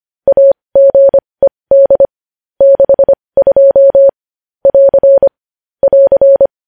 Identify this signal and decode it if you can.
Morse code, heard as a keyed tone, about 25 wpm, spaced as for 20 wpm, 550 Hz. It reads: AGED 62 <AR> <AR>
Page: news-headlines-in-morse-code-at-20-wpm